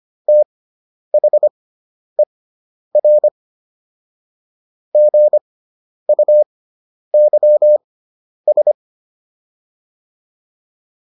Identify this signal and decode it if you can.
Morse code, heard as a keyed tone, about 25 wpm, spaced as for 10 wpm, 600 Hz. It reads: THER GUYS